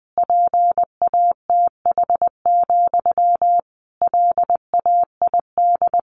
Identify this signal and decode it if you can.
Morse code, heard as a keyed tone, about 20 wpm, 700 Hz. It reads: PATH, LAID
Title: A Sound Of Thunder 20wpm